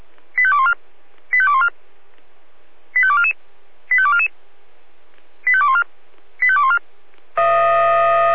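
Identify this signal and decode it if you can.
Five-tone selective call (ZVEI), heard as quick runs of five tones, tones of about 70 ms. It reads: 85214 85214 85244 85244 85214 85214
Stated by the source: ZVEI five-tone call received on SDR.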